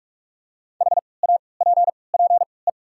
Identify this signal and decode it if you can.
Morse code, heard as a keyed tone, about 43 wpm, spaced as for 25 wpm, 690 Hz. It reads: HAPPE
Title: 3x Character Spacing - Top 500 Words - 45wpm